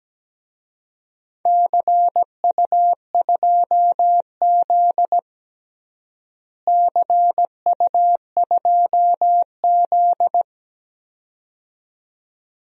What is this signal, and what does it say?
Morse code, heard as a keyed tone, about 17 wpm, 700 Hz. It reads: CU2Z CU2Z
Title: DXCC-100 Call Signs - 4 Characters - Repeated 2x - 17wpm